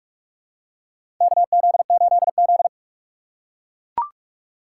Morse code, heard as a keyed tone, about 42 wpm, 700 Hz. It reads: KZ8Z
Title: USA Call Signs - 4 Characters - 4x Character Spacing - 45wpm